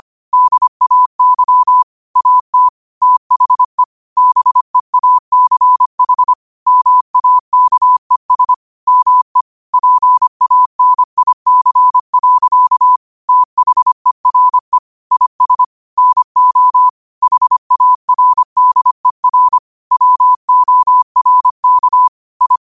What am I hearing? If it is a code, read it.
Morse code, heard as a keyed tone, about 25 wpm, 1000 Hz. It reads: DAY AT THE BEACH MAKES ME PANIC. THERE IS NO HARDER WORK I
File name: qotdfiles / 25wpm